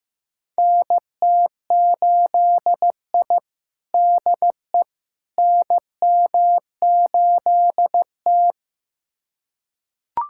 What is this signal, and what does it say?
Morse code, heard as a keyed tone, about 15 wpm, 700 Hz. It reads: NT8I DE NM8T
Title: USA Callsign DE Callsign - 4 Characters - 15wpm